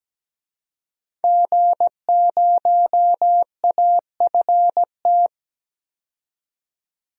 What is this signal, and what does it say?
Morse code, heard as a keyed tone, about 17 wpm, 700 Hz. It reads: G0AFT
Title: DXCC-100 Call Signs - 5 Characters - 17wpm